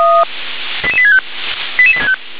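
Five-tone selective call (ZVEI), heard as quick runs of five tones, tones of about 70 ms. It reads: 88865
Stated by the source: ZVEI five-tone selective call 88842, received by SDR.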